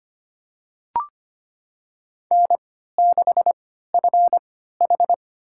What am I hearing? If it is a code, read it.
Morse code, heard as a keyed tone, about 25 wpm, spaced as for 14 wpm, 700 Hz. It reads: N6FH
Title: USA Call Signs - 4 Characters - 3x Character Spacing - 25wpm